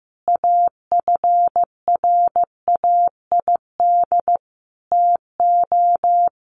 Morse code, heard as a keyed tone, about 14 wpm, 700 Hz. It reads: AFRAID TO
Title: Sentences from Top 1500 Words 15wpm